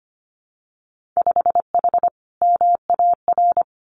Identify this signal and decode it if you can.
Morse code, heard as a keyed tone, about 25 wpm, 700 Hz. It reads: 5H MAR
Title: Winter Field Day Exchanges Encore 25wpm